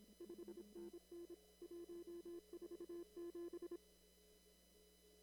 Morse code, heard as a keyed tone, about 26 wpm, 340 Hz. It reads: 5NN 147
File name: CW / kt5j wpx cw-op-k5pi